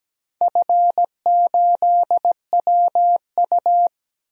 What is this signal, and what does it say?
Morse code, heard as a keyed tone, about 17 wpm, 700 Hz. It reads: F8WU
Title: DXCC-100 Call Signs - 4 Characters - Repeated 3x - 17wpm